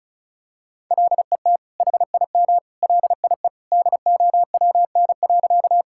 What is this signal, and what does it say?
Morse code, heard as a keyed tone, about 36 wpm, 700 Hz. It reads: LET HIM LIE DOWN.
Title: Sentences from Top 1600 Words 35wpm